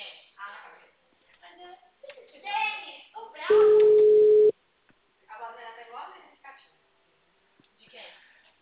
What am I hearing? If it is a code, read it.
Ringback tone (Europe, 425 Hz), with speech.
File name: out- -2004- - - .50851